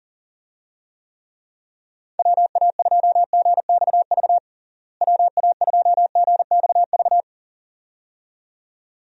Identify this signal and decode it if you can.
Morse code, heard as a keyed tone, about 40 wpm, 700 Hz. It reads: WA2GXV WA2GXV
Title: USA Call Signs - 6 Characters - Repeated 2x - 40wpm